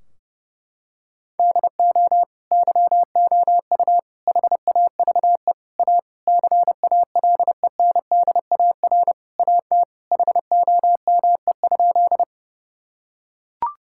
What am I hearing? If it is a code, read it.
Morse code, heard as a keyed tone, about 32 wpm, 700 Hz. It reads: DO YOU HAVE A CALENDAR AT HOME?
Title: Sentences from Top 1600 Words 30wpm